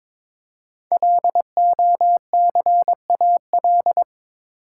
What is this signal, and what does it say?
Morse code, heard as a keyed tone, about 22 wpm, 700 Hz. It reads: LOCAL
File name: 5 Letter Words - First 3 Letters Given - 22wpm